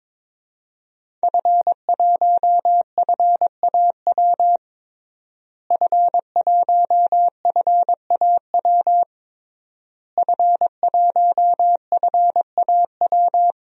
Morse code, heard as a keyed tone, about 22 wpm, 700 Hz. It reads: F1FAW F1FAW F1FAW
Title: DXCC-100 Call Signs - 5 Characters - Repeated 3x - 22wpm